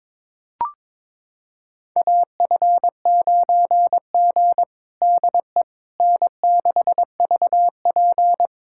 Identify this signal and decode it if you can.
Morse code, heard as a keyed tone, about 22 wpm, 700 Hz. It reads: AF9G DE N6VP
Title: USA Callsign DE Callsign - 4 Characters - 22wpm